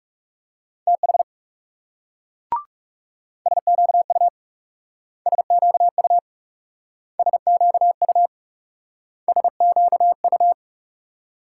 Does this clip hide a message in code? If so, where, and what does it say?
Yes: Morse code from 0.9 to 10.5 s, heard as a keyed tone, about 38 wpm, 700 Hz. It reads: TH SQU SQU SQU SQU